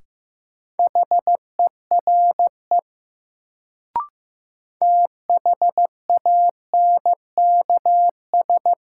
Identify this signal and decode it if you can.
Morse code, heard as a keyed tone, about 15 wpm, 700 Hz. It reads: HERE THANKS
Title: Sets of 5 Words - New Word Drill - 15wpm